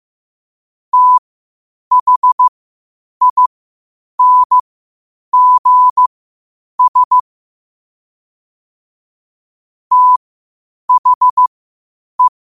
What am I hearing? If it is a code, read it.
Morse code, heard as a keyed tone, about 15 wpm, spaced as for 8 wpm, 1000 Hz. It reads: THINGS THE
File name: qotdfiles / 8wpm